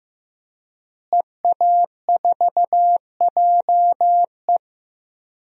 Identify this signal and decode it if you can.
Morse code, heard as a keyed tone, about 15 wpm, 700 Hz. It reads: EA4JE
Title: DXCC-100 Call Signs - 5 Characters - Repeated 2x - 15wpm